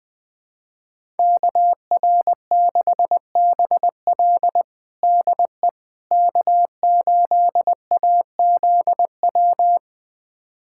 Morse code, heard as a keyed tone, about 21 wpm, 700 Hz. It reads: KR6BL DE K8AZW
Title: USA Callsign DE Callsign - 5 Characters - 20wpm